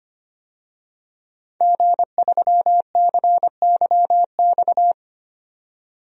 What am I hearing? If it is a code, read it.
Morse code, heard as a keyed tone, about 25 wpm, 700 Hz. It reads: G3CYX
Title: DXCC-100 Call Signs - 5 Characters - 25wpm